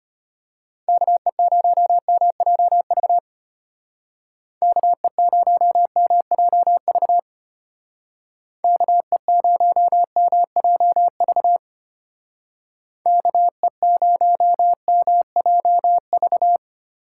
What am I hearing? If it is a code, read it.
Morse code, heard as a keyed tone, about 31 wpm, 700 Hz. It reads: KE0MJV KE0MJV KE0MJV KE0MJV
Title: Speed Racing - USA Call Signs 25wpm